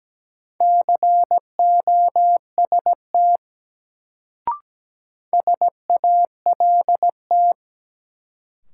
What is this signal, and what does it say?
Morse code, heard as a keyed tone, about 17 wpm, 700 Hz. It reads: COST SALT
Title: Sentences from Top 800 Words - New Word Drill - 17wpm